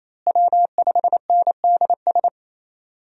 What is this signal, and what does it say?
Morse code, heard as a keyed tone, about 28 wpm, 700 Hz. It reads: W5NDS
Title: USA Call Signs - 5 Characters - Repeated 2x - 28wpm